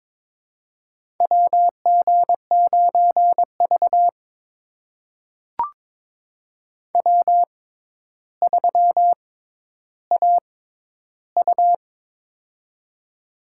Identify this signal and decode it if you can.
Morse code, heard as a keyed tone, about 23 wpm, spaced as for 11 wpm, 700 Hz. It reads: WG9V W3AU
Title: USA Call Signs - 4 Characters - 6x Character Spacing - 22wpm